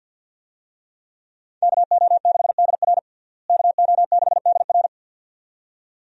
Morse code, heard as a keyed tone, about 48 wpm, 700 Hz. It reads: KO6DR KO6DR
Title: USA Call Signs - 5 Characters - Repeated 2x - 50wpm